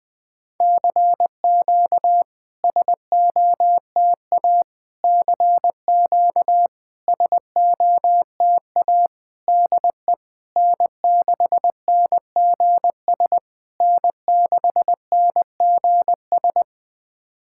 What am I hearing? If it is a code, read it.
Morse code, heard as a keyed tone, about 21 wpm, 700 Hz. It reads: CQ SOTA CQ SOTA DE N6NGS N6NGS
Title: SOTA QSO Practice 20wpm